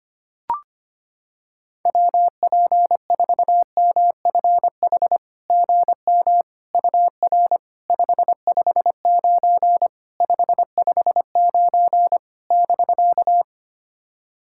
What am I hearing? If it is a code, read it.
Morse code, heard as a keyed tone, about 25 wpm, 700 Hz. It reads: WP4MFH GM UR 559 559 <BK>
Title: SOTA QSO Practice 25wpm